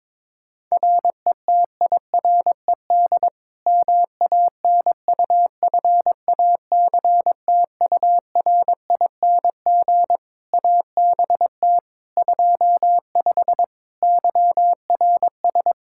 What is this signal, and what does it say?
Morse code, heard as a keyed tone, about 22 wpm, 700 Hz. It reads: RETIRED MANUFACTURING ABT 25 YRS
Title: K1USN SST QSO Practice 22wpm